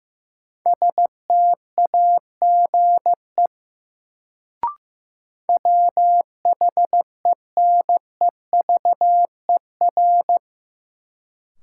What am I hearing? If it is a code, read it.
Morse code, heard as a keyed tone, about 15 wpm, 700 Hz. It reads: STAGE WHENEVER
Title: Sentences from Top 1200 Words - New Word Drill - 15wpm